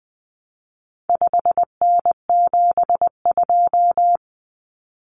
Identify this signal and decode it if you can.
Morse code, heard as a keyed tone, about 20 wpm, 700 Hz. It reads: 5N72
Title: Common Cut Numbers - 4-Digit 20wpm